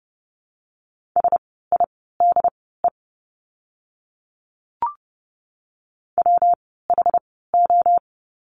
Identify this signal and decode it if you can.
Morse code, heard as a keyed tone, about 30 wpm, spaced as for 17 wpm, 700 Hz. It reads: SIDE WHO
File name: course / 30:10wpm / 115 Lesson 19f - Letter w - Words 30wpm